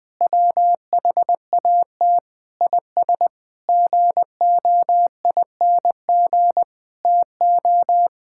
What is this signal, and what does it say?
Morse code, heard as a keyed tone, about 19 wpm, 700 Hz. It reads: WHAT IS GOING TO
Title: Sentences with 7-Word Phrases - 20wpm